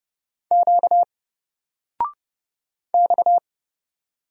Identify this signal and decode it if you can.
Morse code, heard as a keyed tone, about 30 wpm, 700 Hz. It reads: Q X
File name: Single-Letter-Number-30wpm